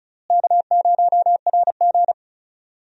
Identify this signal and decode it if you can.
Morse code, heard as a keyed tone, about 36 wpm, 700 Hz. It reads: K0RG
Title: USA Call Signs - 4 Characters - Repeated 2x - 35wpm